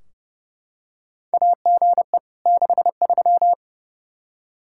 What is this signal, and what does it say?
Morse code, heard as a keyed tone, about 30 wpm, 700 Hz. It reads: AGE 63
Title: Practice QSO - Exchange 3 - More Info - 30wpm